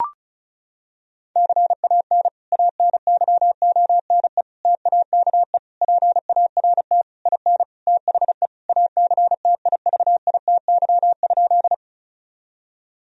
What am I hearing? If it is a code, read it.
Morse code, heard as a keyed tone, about 36 wpm, 700 Hz. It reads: CAN ANYONE TAKE PART IN THE ACTIVITY?
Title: Sentences from Top 1000 Words - Review Part 4 - 35wpm